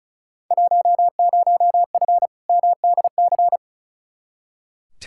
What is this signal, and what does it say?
Morse code, heard as a keyed tone, about 36 wpm, 700 Hz. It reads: 10F MDC
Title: Field Day Exchanges Encore - 35wpm